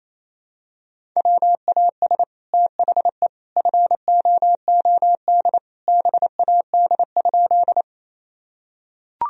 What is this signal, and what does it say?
Morse code, heard as a keyed tone, about 27 wpm, 700 Hz. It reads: WAS THE FOOD BAD?
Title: Sentences from Top 1000 Words - Review Part 1 - 28wpm